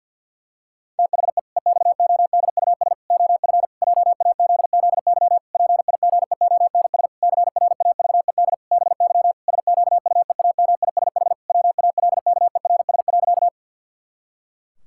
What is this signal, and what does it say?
Morse code, heard as a keyed tone, about 50 wpm, 700 Hz. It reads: THE EXODUS OF JAZZY PIGEONS CRAVED BY SQUEAMISH WALKERS.
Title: Pangram Challenge v2 - 50wpm